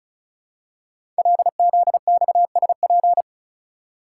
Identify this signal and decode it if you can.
Morse code, heard as a keyed tone, about 36 wpm, 700 Hz. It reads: LZXSP
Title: Code Groups - 5 Letters - 35wpm